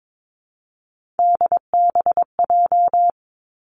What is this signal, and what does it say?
Morse code, heard as a keyed tone, about 22 wpm, 700 Hz. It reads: DBJ
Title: Three Challenging Letter-Number Combinations 22wpm